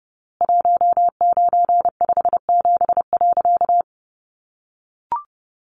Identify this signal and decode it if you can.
Morse code, heard as a keyed tone, about 30 wpm, 700 Hz. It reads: 1957.
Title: A Sound Of Thunder 30wpm